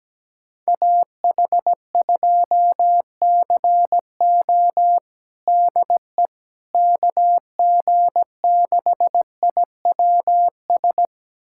Morse code, heard as a keyed tone, about 17 wpm, 700 Hz. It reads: AH2CO DE KG6IWS
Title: USA Callsign DE Callsign 17wpm